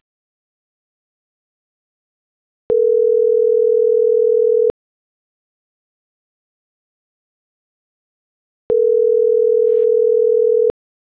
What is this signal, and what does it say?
Ringback tone (North America), 440 + 480 Hz.